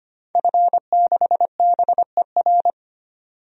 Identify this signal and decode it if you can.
Morse code, heard as a keyed tone, about 25 wpm, 700 Hz. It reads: F6BER
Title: DXCC-100 Call Signs - 5 Characters - 25wpm